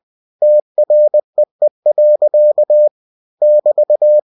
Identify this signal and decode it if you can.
Morse code, heard as a keyed tone, about 20 wpm, 600 Hz. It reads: TREE. <BT>
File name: cw-books / winnie-the-pooh / winnie-20wpm-page-0025